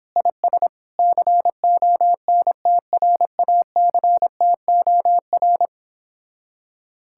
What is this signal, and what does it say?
Morse code, heard as a keyed tone, about 29 wpm, 700 Hz. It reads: IS CONTRACTOR